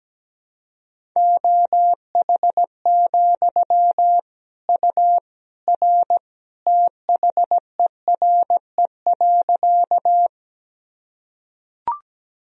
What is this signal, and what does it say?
Morse code, heard as a keyed tone, about 17 wpm, 700 Hz. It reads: OH, U R THERE.
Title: Sentences from Top 500 Words and Abbreviations 17wpm